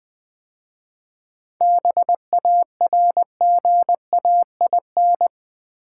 Morse code, heard as a keyed tone, about 20 wpm, 700 Hz. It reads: BARGAIN
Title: 7 Letter Words - First 3 Letters Given - 20wpm